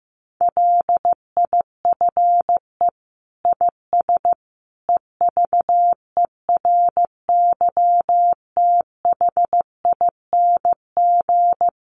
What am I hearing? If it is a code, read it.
Morse code, heard as a keyed tone, about 15 wpm, 700 Hz. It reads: LIFE IS EVERYTHING
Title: Sentences from Top 700 Words 15wpm